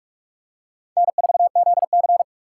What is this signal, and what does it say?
Morse code, heard as a keyed tone, about 44 wpm, 700 Hz. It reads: N4ZC
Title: USA Call Signs - 4 Characters - 3x Character Spacing - 45wpm